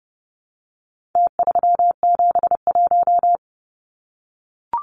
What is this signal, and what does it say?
Morse code, heard as a keyed tone, about 30 wpm, 700 Hz. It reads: T371
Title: Numbers - Contest Style - 4-Digit 30wpm